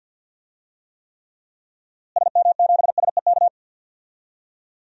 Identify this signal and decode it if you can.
Morse code, heard as a keyed tone, about 51 wpm, 700 Hz. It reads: SM7SEK
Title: DXCC-100 Call Signs - 6 Characters - Repeated 3x - 50wpm